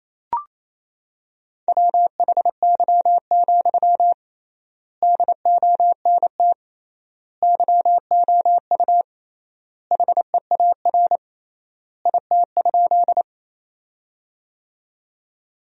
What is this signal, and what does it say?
Morse code, heard as a keyed tone, about 27 wpm, 700 Hz. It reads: WHY, DONT YOU HEAR IT?
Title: 3x Word-Spacing - Sentences from Top 300 Words - 28wpm